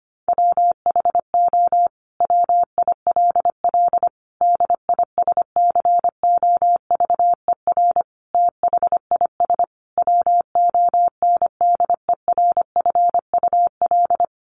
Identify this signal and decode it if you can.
Morse code, heard as a keyed tone, about 25 wpm, 700 Hz. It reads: WHO WILL DISCOVER THIS WONDERFUL
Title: Sentences from Top 1100 Words 25wpm